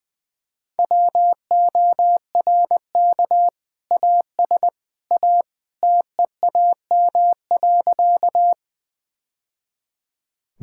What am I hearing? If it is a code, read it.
Morse code, heard as a keyed tone, about 20 wpm, 700 Hz. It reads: WORK AS A TEAM.
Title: Sentences with 2-Word Phrases - 20wpm